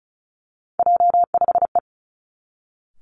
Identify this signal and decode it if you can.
Morse code, heard as a keyed tone, about 36 wpm, 700 Hz. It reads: J5E
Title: Three Challenging Letter-Number Combinations 35wpm